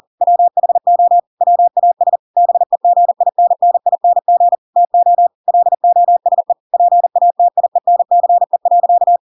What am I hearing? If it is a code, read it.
Morse code, heard as a keyed tone, about 40 wpm, 700 Hz. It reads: WHO WAS BEGINNING TO LOSE PATIENCE.
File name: Pinocchio / Pinocchio ch4 40 40 ik1hge